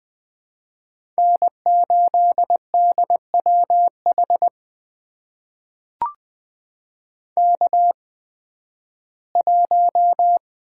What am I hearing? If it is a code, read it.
Morse code, heard as a keyed tone, about 20 wpm, spaced as for 13 wpm, 700 Hz. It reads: N8DWH K1
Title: USA Call Signs - 5 Characters - 8x Character Spacing - 20wpm